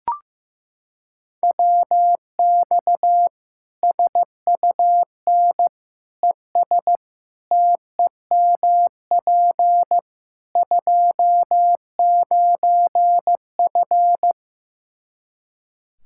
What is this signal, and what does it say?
Morse code, heard as a keyed tone, about 16 wpm, 700 Hz. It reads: WX SUN ES TEMP 29F
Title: K1USN SST QSO Practice 15wpm